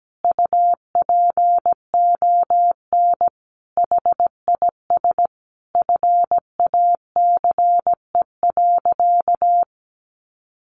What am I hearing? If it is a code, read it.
Morse code, heard as a keyed tone, about 17 wpm, 700 Hz. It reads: UPON HIS FACE.
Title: Sentences from Top 1100 Words 17wpm